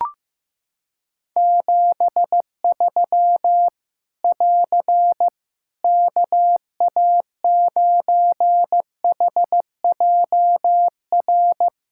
Morse code, heard as a keyed tone, about 15 wpm, 700 Hz. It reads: 73 <AR> KA9HJR
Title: Practice QSO - Exchange 4 - Ending - 15wpm